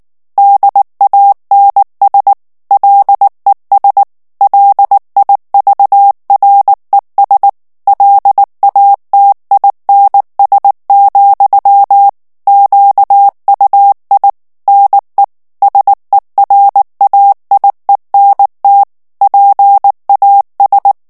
Morse code, heard as a keyed tone, about 19 wpm, 800 Hz. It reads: DANS LES LIVRES LATINS, QUI NE SERAIENT PAS
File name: CW / FICHIERS-MP3 / FRANCAIS TEXTE CLAIR 2 VITESSE 18 MOTS MINUTES